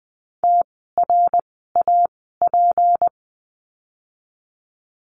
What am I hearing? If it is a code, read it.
Morse code, heard as a keyed tone, about 19 wpm, spaced as for 14 wpm, 700 Hz. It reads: TRAP
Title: 153 Lesson 25f - Letter p - Words 20wpm